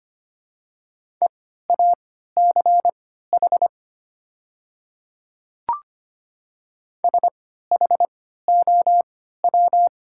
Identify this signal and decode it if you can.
Morse code, heard as a keyed tone, about 25 wpm, spaced as for 14 wpm, 700 Hz. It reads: EACH SHOW
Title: 3x Character Spacing - Top 200 Words - 25wpm